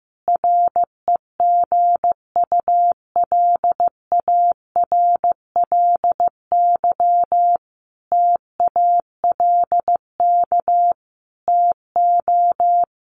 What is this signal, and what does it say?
Morse code, heard as a keyed tone, about 15 wpm, 700 Hz. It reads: REGULARLY TALK TO
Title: Sentences from Top 1400 Words 15wpm